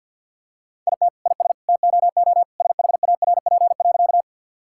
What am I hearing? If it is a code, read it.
Morse code, heard as a keyed tone, about 50 wpm, 700 Hz. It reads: IT IS TOO SHARP.
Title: Sentences from Top 1600 Words 50wpm